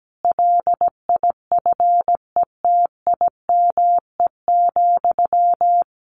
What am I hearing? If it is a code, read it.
Morse code, heard as a keyed tone, about 17 wpm, 700 Hz. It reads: LIFETIME,
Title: A Sound Of Thunder 17wpm